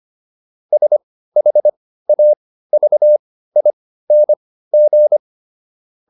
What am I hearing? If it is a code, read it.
Morse code, heard as a keyed tone, about 25 wpm, spaced as for 15 wpm, 600 Hz. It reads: SHAVING